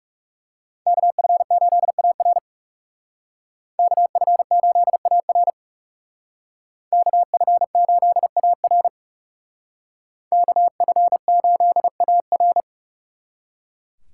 Morse code, heard as a keyed tone, about 39 wpm, 700 Hz. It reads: KF8AR KF8AR KF8AR KF8AR